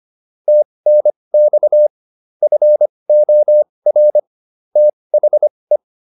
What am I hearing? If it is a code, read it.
Morse code, heard as a keyed tone, about 25 wpm, spaced as for 20 wpm, 600 Hz. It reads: TNX FOR THE